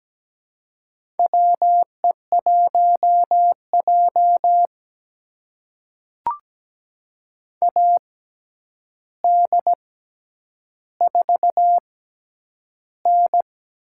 Morse code, heard as a keyed tone, about 17 wpm, spaced as for 9 wpm, 700 Hz. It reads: WE1J AD4N
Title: USA Call Signs - 4 Characters - 6x Character Spacing - 17wpm